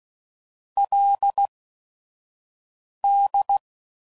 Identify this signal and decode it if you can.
Morse code, heard as a keyed tone, about 16 wpm, spaced as for 5 wpm, 800 Hz. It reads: LD
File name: CW Code Groups @5WPM / Technical Subjects / AND Gate